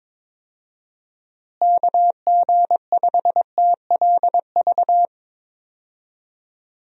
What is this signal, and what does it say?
Morse code, heard as a keyed tone, about 22 wpm, 700 Hz. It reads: KG5TLV
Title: USA Call Signs - 6 Characters - Repeated 3x - 22wpm